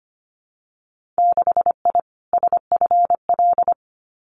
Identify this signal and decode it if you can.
Morse code, heard as a keyed tone, about 25 wpm, 700 Hz. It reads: 6I SFL
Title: Winter Field Day Exchanges 25wpm